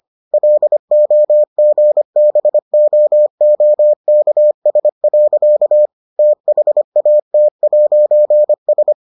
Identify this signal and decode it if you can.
Morse code, heard as a keyed tone, about 25 wpm, 600 Hz. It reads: LOGBOOKS. THAT'S